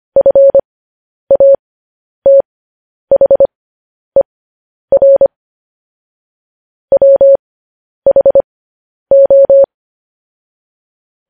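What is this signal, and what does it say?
Morse code, heard as a keyed tone, about 25 wpm, spaced as for 10 wpm, 550 Hz. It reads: FATHER WHO
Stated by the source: Morse code news headlines broadcast.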